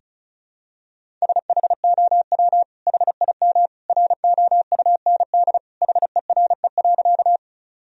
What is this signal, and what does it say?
Morse code, heard as a keyed tone, about 35 wpm, 700 Hz. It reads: SHOW HIM ROUND HERE.
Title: Sentences from Top 1000 Words - Review Part 1 - 35wpm